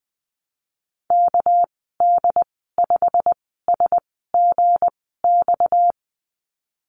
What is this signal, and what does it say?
Morse code, heard as a keyed tone, about 19 wpm, 700 Hz. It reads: K D 5 S G X
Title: 238 Lesson 38g - Letter q - Call Signs 20wpm